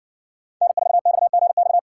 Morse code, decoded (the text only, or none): N4XGX